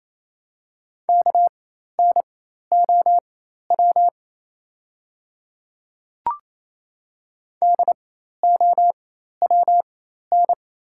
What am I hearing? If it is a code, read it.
Morse code, heard as a keyed tone, about 27 wpm, spaced as for 13 wpm, 690 Hz. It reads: KNOW DOWN
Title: 4x Character Spacing - Top 200 Words - 28wpm